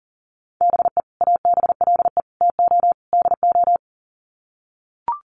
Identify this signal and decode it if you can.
Morse code, heard as a keyed tone, about 40 wpm, 700 Hz. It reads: BE ABLE TO DO
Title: Sets of 4 Words 40wpm